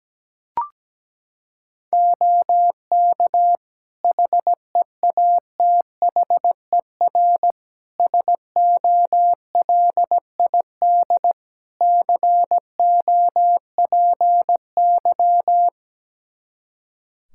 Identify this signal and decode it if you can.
Morse code, heard as a keyed tone, about 17 wpm, 700 Hz. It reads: OK HEATHER SOLID COPY
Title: K1USN SST QSO Practice 17wpm